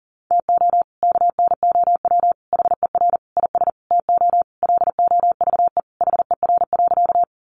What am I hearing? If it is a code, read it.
Morse code, heard as a keyed tone, about 40 wpm, 700 Hz. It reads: TO KNOW HER IS TO LOVE HER.
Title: Sentences from Top 400 Words 40wpm